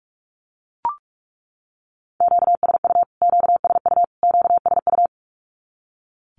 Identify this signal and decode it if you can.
Morse code, heard as a keyed tone, about 44 wpm, 700 Hz. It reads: QSU QSU QSU
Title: Q-Codes Repeated 3x 45wpm